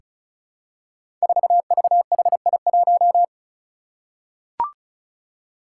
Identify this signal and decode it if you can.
Morse code, heard as a keyed tone, about 36 wpm, 700 Hz. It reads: 4VHI1
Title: Five Challenging Letter-Number Combinations 35wpm